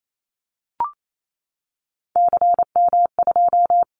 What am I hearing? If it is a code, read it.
Morse code, heard as a keyed tone, about 28 wpm, 700 Hz. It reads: CM2
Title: Callsign Prefixes for Top 100 DXCC - 28wpm